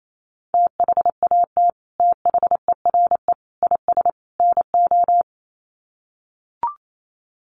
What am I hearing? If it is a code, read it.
Morse code, heard as a keyed tone, about 28 wpm, 700 Hz. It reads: THAT THERE IS NO
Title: Sets of 4 Words 28wpm